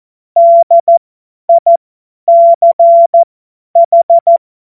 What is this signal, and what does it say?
Morse code, heard as a keyed tone, about 14 wpm, spaced as for 10 wpm, 680 Hz. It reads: DICH
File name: Morsekurs / Witze / Witz 25. Oktober 2022 50 F70